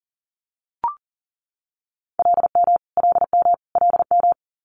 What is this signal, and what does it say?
Morse code, heard as a keyed tone, about 38 wpm, 700 Hz. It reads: LM LM LM
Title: Multi-letter Phonemes Repeated 3x 40wpm